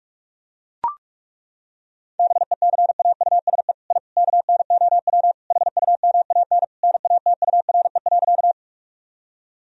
Morse code, heard as a keyed tone, about 41 wpm, 700 Hz. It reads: BECAUSE I KNOW HUMAN NATURE.
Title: Sentences from Top 1000 Words - Review Part 3 - 45wpm